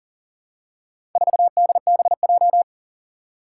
Morse code, heard as a keyed tone, about 40 wpm, 700 Hz. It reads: 4DBJ
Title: Four Challenging Letter-Number Combinations 40wpm